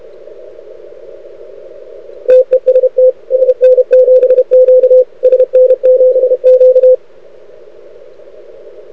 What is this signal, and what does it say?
Morse code, heard as a keyed tone, about 31 wpm, 500 Hz. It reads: TEST SN7Q SN7Q
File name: rec contest / arrl cw 2018 / test 32